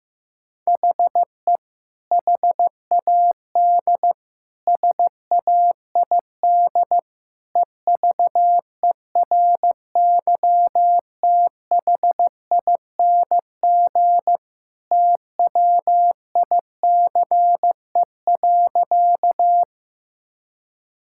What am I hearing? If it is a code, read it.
Morse code, heard as a keyed tone, about 16 wpm, 700 Hz. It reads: HE HAD SAID EVERYTHING TWICE.
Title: Sentences from Top 1600 Words 15wpm